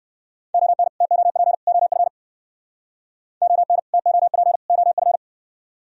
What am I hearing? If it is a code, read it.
Morse code, heard as a keyed tone, about 57 wpm, 700 Hz. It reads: ON TOP OF ON TOP OF